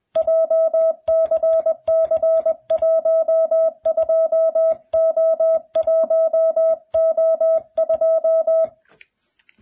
Morse code, heard as a keyed tone, about 21 wpm, 660 Hz. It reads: JCC12O1O2